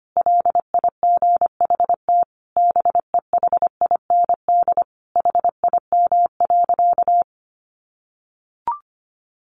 Morse code, heard as a keyed tone, about 25 wpm, 700 Hz. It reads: LIGHT BEHIND HIM.
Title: Sentences from Top 1000 Words 25wpm